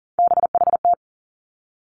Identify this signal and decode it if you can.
Morse code, heard as a keyed tone, about 41 wpm, 700 Hz. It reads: BHT
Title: Three Challenging Letter-Number Combinations 40wpm